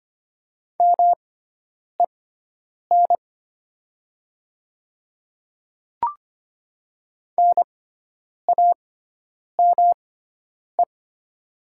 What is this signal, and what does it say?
Morse code, heard as a keyed tone, about 25 wpm, spaced as for 9 wpm, 700 Hz. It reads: MEN NAME
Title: 6x Character Spacing - Top 300 Words - 25wpm